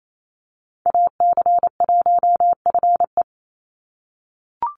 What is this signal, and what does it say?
Morse code, heard as a keyed tone, about 28 wpm, 700 Hz. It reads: AC1FE
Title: Call Signs - 28wpm